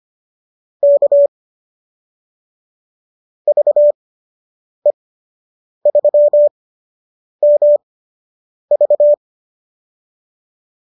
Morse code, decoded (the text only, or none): K VE3MV